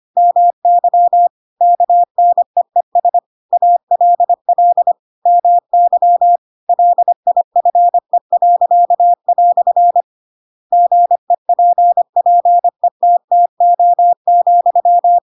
Morse code, heard as a keyed tone, about 25 wpm, 700 Hz. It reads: MY KNEES ALL MY LIFE." GEPPETTO,
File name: Pinocchio / Pinocchio ch7 25 25 ik1hge